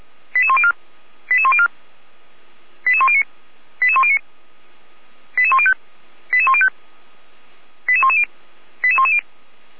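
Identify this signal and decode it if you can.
Five-tone selective call (ZVEI), heard as quick runs of five tones, tones of about 70 ms. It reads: 88104 88104 88109 88109 88105 88105 88110 88110
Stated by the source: Five-tone ZVEI selective call received by SDR, from feuerwehr-orlamuende.org (Orlamünde fire brigade).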